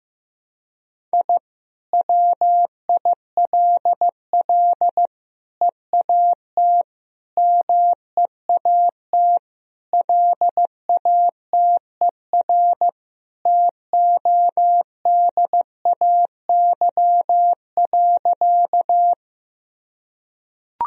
Morse code, heard as a keyed tone, about 16 wpm, 700 Hz. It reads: I WILL EAT MEAT LATER TODAY.
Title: Sentences from Top 1000 Words - Review Part 3 - 15wpm